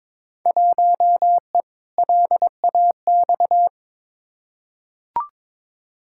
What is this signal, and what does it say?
Morse code, heard as a keyed tone, about 22 wpm, 700 Hz. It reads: 1E LAX
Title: Field Day Exchanges - 22wpm